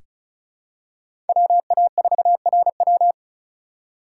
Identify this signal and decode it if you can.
Morse code, heard as a keyed tone, about 36 wpm, 700 Hz. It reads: WA4RW
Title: USA Call Signs - 5 Characters - 3x Character Spacing - 35wpm